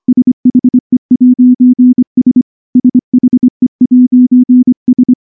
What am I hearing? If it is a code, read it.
Morse code, heard as a keyed tone, about 23 wpm, 250 Hz. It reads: SHE'S SHE'S
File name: morse / morse-training-25WPM